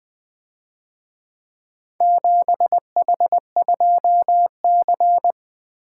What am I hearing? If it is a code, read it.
Morse code, heard as a keyed tone, about 24 wpm, 700 Hz. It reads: 7H2C
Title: Speed-Racing - Four Letter-Number Combinations - 20wpm